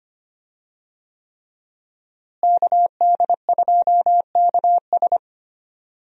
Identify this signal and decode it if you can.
Morse code, heard as a keyed tone, about 25 wpm, 700 Hz. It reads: KD2KS
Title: USA Call Signs - 5 Characters - Repeated 3x - 25wpm